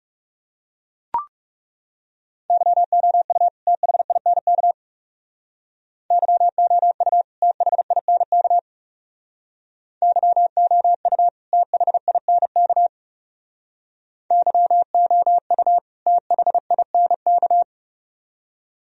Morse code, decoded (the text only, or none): YOU THINK YOU THINK YOU THINK YOU THINK